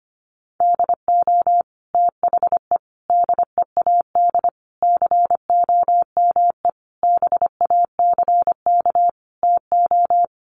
Morse code, heard as a keyed tone, about 25 wpm, 700 Hz. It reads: DO THE DEAD COME BACK TO
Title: Sentences from Top 1500 Words 25wpm